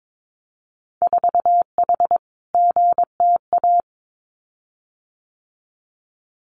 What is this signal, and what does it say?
Morse code, heard as a keyed tone, about 22 wpm, 700 Hz. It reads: 4H GTA
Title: Winter Field Day Exchanges Encore 22wpm